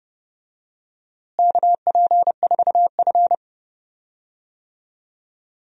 Morse code, heard as a keyed tone, about 30 wpm, 700 Hz. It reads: KP4F
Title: DXCC-100 Call Signs - 4 Characters - Repeated 2x - 30wpm